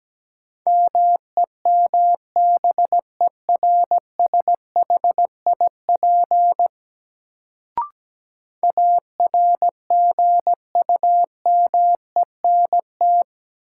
Morse code, heard as a keyed tone, about 17 wpm, 700 Hz. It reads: MEMBERSHIP ARGUMENT
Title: Sentences from Top 1400 Words - New Word Drill - 17wpm